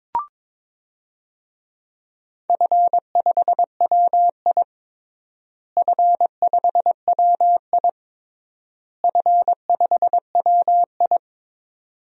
Morse code, heard as a keyed tone, about 22 wpm, 700 Hz. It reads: F5WI F5WI F5WI
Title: DXCC-100 Call Signs - 4 Characters - Repeated 3x - 22wpm